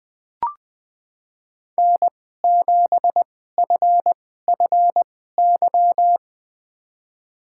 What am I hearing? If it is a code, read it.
Morse code, heard as a keyed tone, about 20 wpm, spaced as for 14 wpm, 700 Hz. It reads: N7FFY
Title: USA Call Signs - 2x Character Spacing - 20wpm